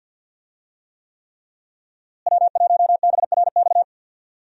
Morse code, heard as a keyed tone, about 52 wpm, 700 Hz. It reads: W1BRX
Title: USA Call Signs - 5 Characters - Repeated 3x - 50wpm